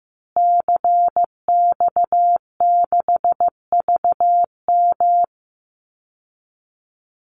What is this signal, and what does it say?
Morse code, heard as a keyed tone, about 15 wpm, 700 Hz. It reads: CX6VM
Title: Call Signs Repeated 2x - 15wpm